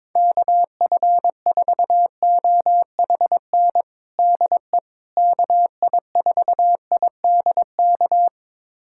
Morse code, heard as a keyed tone, about 21 wpm, 700 Hz. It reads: KF4OHN DE KI4IDK
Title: USA Callsign DE Callsign - 6 Characters - 22wpm